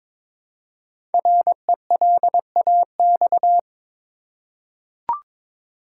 Morse code, heard as a keyed tone, about 22 wpm, 700 Hz. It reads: RELAX
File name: 5 Letter Words - First 3 Letters Given - 22wpm